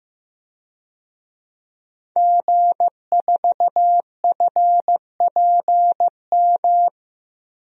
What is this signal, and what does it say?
Morse code, heard as a keyed tone, about 15 wpm, 700 Hz. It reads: G4FPM
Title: DXCC-100 Call Signs - 5 Characters - Repeated 2x - 15wpm